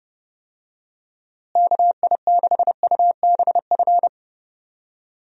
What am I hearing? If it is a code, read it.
Morse code, heard as a keyed tone, about 31 wpm, 700 Hz. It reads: KI6UBF
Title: USA Call Signs - 6 Characters - Repeated 2x - 30wpm